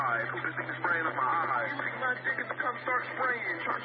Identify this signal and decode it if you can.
Emergency Alert System (SAME) tones, heard as harsh buzzing bursts, with speech. It is heard from 0.4 to 3.6 s.